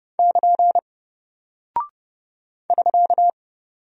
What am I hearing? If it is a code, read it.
Morse code, heard as a keyed tone, about 31 wpm, 700 Hz. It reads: <KN> <SK>
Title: Single Character (Letter:Number:Punctuation:Prosign) Alt 30wpm